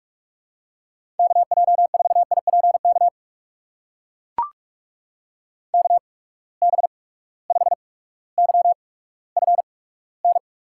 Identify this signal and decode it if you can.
Morse code, heard as a keyed tone, about 44 wpm, spaced as for 19 wpm, 700 Hz. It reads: KJ4IPK KB5YFN
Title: USA Call Signs - 6 Characters - 8x Character Spacing - 45wpm